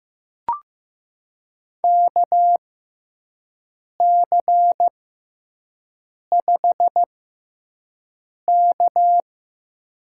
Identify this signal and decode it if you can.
Morse code, heard as a keyed tone, about 15 wpm, spaced as for 5 wpm, 700 Hz. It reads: KC5K
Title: USA Call Signs - 6 Characters - 6x Character Spacing - 15wpm